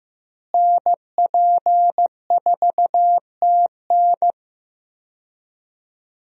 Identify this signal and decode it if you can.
Morse code, heard as a keyed tone, about 15 wpm, 700 Hz. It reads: NP4TN
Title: USA Call Signs - 5 Characters - Repeated 2x - 15wpm